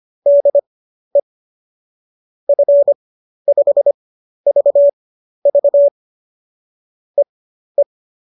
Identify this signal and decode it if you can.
Morse code, heard as a keyed tone, about 25 wpm, spaced as for 12 wpm, 570 Hz. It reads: DE F5VV EE
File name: qso009 12